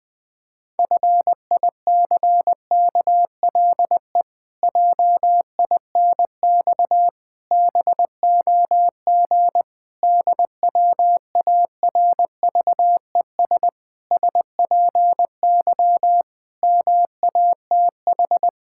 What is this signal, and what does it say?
Morse code, heard as a keyed tone, about 21 wpm, 700 Hz. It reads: FICKLE JINX BOG DWARVES SPY MATH
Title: Pangram Challenge v2 - 20wpm